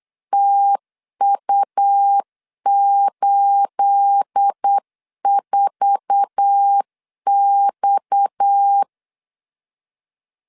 Morse code, heard as a keyed tone, about 8 wpm, 800 Hz. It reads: TU84X
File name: LEZIONI / 9 28 8wpm 3512